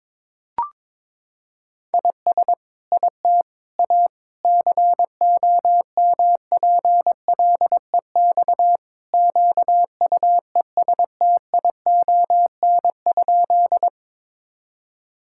Morse code, heard as a keyed tone, about 22 wpm, 700 Hz. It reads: IS IT A COMPLEX QUESTION?
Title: Sentences from Top 1000 Words - Review Part 6 - 22wpm